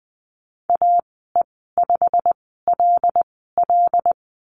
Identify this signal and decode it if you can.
Morse code, heard as a keyed tone, about 20 wpm, spaced as for 15 wpm, 700 Hz. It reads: AE5LL
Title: 161 Lesson 26g - Letter g - Call Signs 20wpm